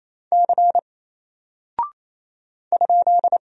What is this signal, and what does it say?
Morse code, heard as a keyed tone, about 27 wpm, 700 Hz. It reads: C ?
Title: Single Character (Letter:Number:Punctuation:Prosign) 28wpm